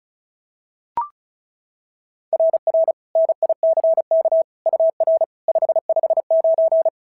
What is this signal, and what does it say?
Morse code, heard as a keyed tone, about 36 wpm, 690 Hz. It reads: RR NICK UR 559
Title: SOTA QSO Practice 35wpm